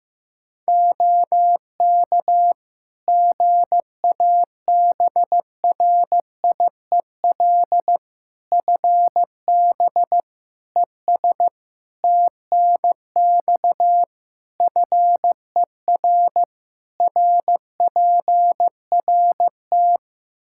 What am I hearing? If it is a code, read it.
Morse code, heard as a keyed tone, about 16 wpm, 700 Hz. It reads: OK GABRIEL FB ES TNX FER RPRT
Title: Practice QSO - Exchange 2 - Info - 15wpm